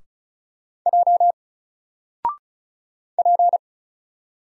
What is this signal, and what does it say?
Morse code, heard as a keyed tone, about 34 wpm, 700 Hz. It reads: J P
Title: Single Letter-Number Alt 35wpm